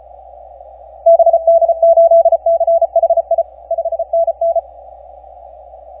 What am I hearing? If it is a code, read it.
Morse code, heard as a keyed tone, about 33 wpm, 670 Hz. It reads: BD8CHI 5NN
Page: 3Y0K 17 Meters Longpath 1400Z